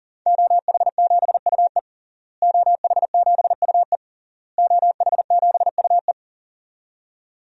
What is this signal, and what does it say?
Morse code, heard as a keyed tone, about 37 wpm, 700 Hz. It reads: OH7UE OH7UE OH7UE
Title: DXCC-100 Call Signs - 5 Characters - Repeated 3x - 40wpm